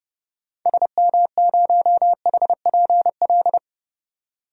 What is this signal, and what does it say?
Morse code, heard as a keyed tone, about 32 wpm, 700 Hz. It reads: SM0HPL
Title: DXCC-100 Call Signs - 6 Characters - 30wpm